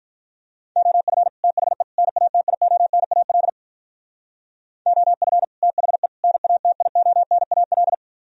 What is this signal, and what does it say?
Morse code, heard as a keyed tone, about 42 wpm, 700 Hz. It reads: OF THE NATIONAL OF THE NATIONAL
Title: Speed-Racing - Sets of 3 Words 35wpm